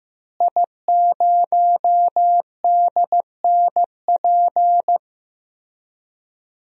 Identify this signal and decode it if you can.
Morse code, heard as a keyed tone, about 15 wpm, 700 Hz. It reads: I0DNP
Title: DXCC-100 Call Signs - 5 Characters - Repeated 3x - 15wpm